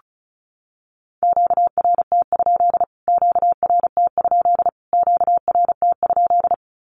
Morse code, heard as a keyed tone, about 34 wpm, 700 Hz. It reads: QRT? QRT? QRT?
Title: Q-Codes Repeated 3x 35wpm